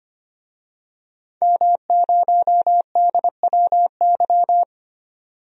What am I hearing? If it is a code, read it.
Morse code, heard as a keyed tone, about 25 wpm, 700 Hz. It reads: M0DWY
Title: DXCC-100 Call Signs - 5 Characters - 25wpm